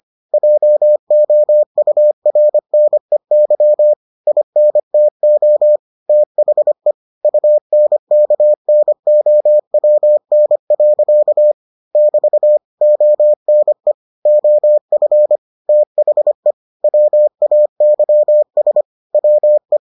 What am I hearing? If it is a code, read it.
Morse code, heard as a keyed tone, about 25 wpm, 600 Hz. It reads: JOURNEY INTO THE UNKNOWN. <BT> ONE OF THE WAYS WE